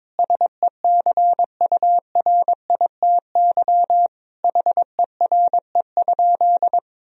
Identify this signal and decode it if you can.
Morse code, heard as a keyed tone, about 22 wpm, 700 Hz. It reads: SECURITY HERE?
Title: Sentences from Top 1000 Words - Review Part 4 - 22wpm